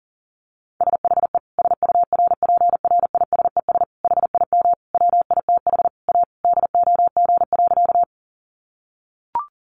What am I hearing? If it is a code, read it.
Morse code, heard as a keyed tone, about 40 wpm, 700 Hz. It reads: SHE SURPRISES HIM WITH A DOG.
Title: Additional Easy Phrases 40wpm